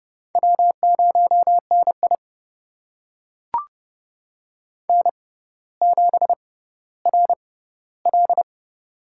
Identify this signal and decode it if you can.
Morse code, heard as a keyed tone, about 30 wpm, spaced as for 15 wpm, 700 Hz. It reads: W0NI N7RL
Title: USA Call Signs - 4 Characters - 6x Character Spacing - 30wpm